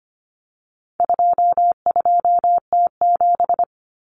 Morse code, heard as a keyed tone, about 25 wpm, 700 Hz. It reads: 22T7
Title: Common Cut Numbers - 4-Digit 25wpm